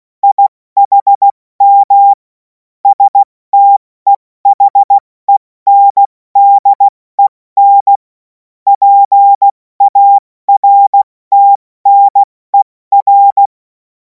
Morse code, heard as a keyed tone, about 16 wpm, 800 Hz. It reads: IHM STEHENDEN PARTNER
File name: Morsekurs / Witze / Witz 17. Februar 2026 70 F80